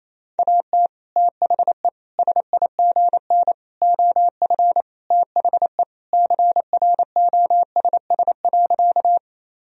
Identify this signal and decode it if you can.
Morse code, heard as a keyed tone, about 28 wpm, 700 Hz. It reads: AT THE SIGN OF THE CROSS.
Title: Sentences from Top 1000 Words - Review Part 1 - 28wpm